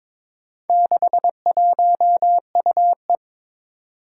Morse code, heard as a keyed tone, about 22 wpm, 700 Hz. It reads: 61UE
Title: Four Challenging Letter-Number Combinations 22wpm